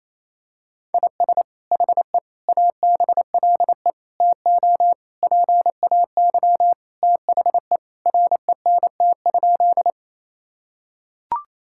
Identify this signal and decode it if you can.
Morse code, heard as a keyed tone, about 28 wpm, 700 Hz. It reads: IS HE ABLE TO PAY THE RENT?
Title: Sentences from Top 1000 Words - Review Part 5 - 28wpm